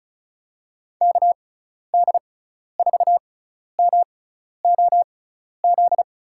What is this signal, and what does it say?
Morse code, heard as a keyed tone, about 36 wpm, spaced as for 12 wpm, 700 Hz. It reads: KD4MOZ
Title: USA Call Signs - 6 Characters - 6x Character Spacing - 35wpm